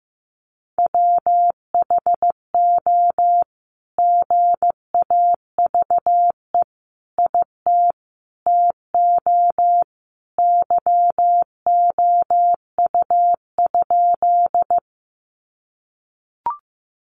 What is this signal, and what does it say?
Morse code, heard as a keyed tone, about 15 wpm, 700 Hz. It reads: WHO GAVE IT TO YOU?
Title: Sentences from Top 500 Words 15wpm